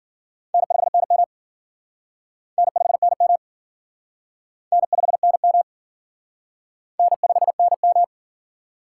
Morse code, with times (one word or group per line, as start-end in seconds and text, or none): 0.5-1.2 N5NM
2.6-3.4 N5NM
4.7-5.6 N5NM
7.0-8.0 N5NM